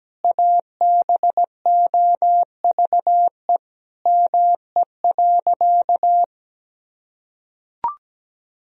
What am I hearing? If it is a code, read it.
Morse code, heard as a keyed tone, about 17 wpm, 700 Hz. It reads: ABOVE ME.
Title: Sentences from Top 1000 Words - Review Part 2 - 17wpm